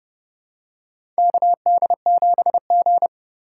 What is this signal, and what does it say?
Morse code, heard as a keyed tone, about 30 wpm, 700 Hz. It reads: KD7G
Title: USA Call Signs - 4 Characters - 4x Character Spacing - 30wpm